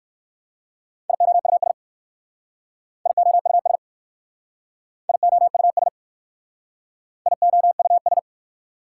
Morse code, heard as a keyed tone, about 58 wpm, 690 Hz. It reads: IOUS IOUS IOUS IOUS